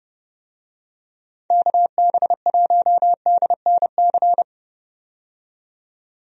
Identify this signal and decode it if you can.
Morse code, heard as a keyed tone, about 30 wpm, 700 Hz. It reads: KB1DNC
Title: USA Call Signs - 6 Characters - 30wpm